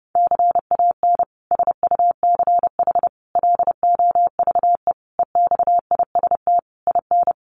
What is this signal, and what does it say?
Morse code, heard as a keyed tone, about 30 wpm, 700 Hz. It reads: CAN SUCH LOVE EXIST IN
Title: Sentences from Top 1100 Words 30wpm